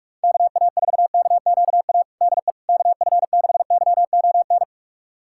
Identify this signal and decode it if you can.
Morse code, heard as a keyed tone, about 46 wpm, 700 Hz. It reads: KA4KQA DE KF6YON
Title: USA Callsign DE Callsign 45wpm